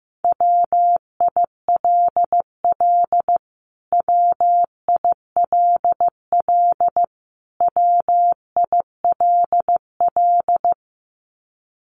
Morse code, heard as a keyed tone, about 15 wpm, 700 Hz. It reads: WILL WILL WILL
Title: Top 200 Words Repeated 15wpm